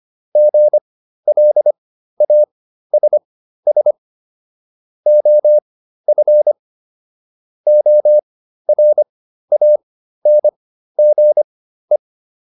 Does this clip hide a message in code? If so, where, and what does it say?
Yes: Morse code from 0.4 to 12.0 s, heard as a keyed tone, about 25 wpm, spaced as for 13 wpm, 600 Hz. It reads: GLASS OF ORANGE